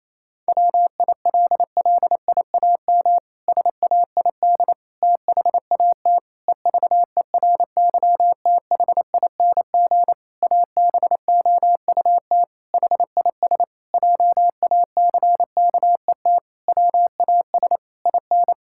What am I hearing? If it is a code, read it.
Morse code, heard as a keyed tone, about 28 wpm, 700 Hz. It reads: WILLIAM SAID THAT EVERYTHING ABOUT HIS JACKET WAS IN
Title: Pangram Challenge v2 - 28wpm